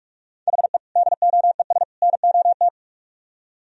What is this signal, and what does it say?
Morse code, heard as a keyed tone, about 57 wpm, 700 Hz. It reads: HE DOES NOT